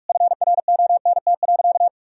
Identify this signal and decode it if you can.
Morse code, heard as a keyed tone, about 46 wpm, 700 Hz. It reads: FRONT.
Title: Sentences from Top 1000 Words - Review Part 2 - 45wpm